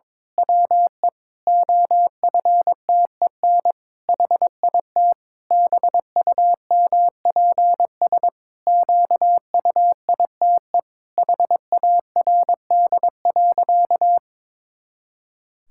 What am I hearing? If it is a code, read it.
Morse code, heard as a keyed tone, about 22 wpm, 700 Hz. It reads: WE OFTEN HIT BUMPS QUITE HARD.
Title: Pangram Challenge v2 - 22wpm